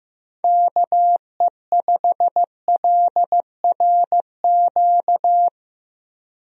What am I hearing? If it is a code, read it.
Morse code, heard as a keyed tone, about 15 wpm, 700 Hz. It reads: KE5LRQ
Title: USA Call Signs - 6 Characters - Repeated 2x - 15wpm